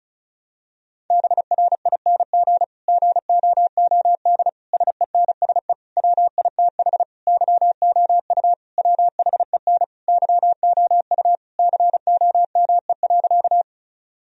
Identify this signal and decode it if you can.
Morse code, heard as a keyed tone, about 36 wpm, 700 Hz. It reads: BRING GOOD SENSE WITH YOU WHEN YOU COME.
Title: Sentences from Top 1000 Words - Review Part 4 - 35wpm